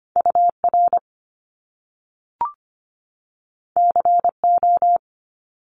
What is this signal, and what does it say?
Morse code, heard as a keyed tone, about 25 wpm, 700 Hz. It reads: UR CO
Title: Top Two Letter Combinations in Words 25wpm